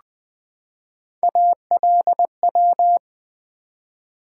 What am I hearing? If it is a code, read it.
Morse code, heard as a keyed tone, about 20 wpm, 700 Hz. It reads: ALW
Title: Build A Word - 3 Letters At A Time - 6-Letter Words - 20wpm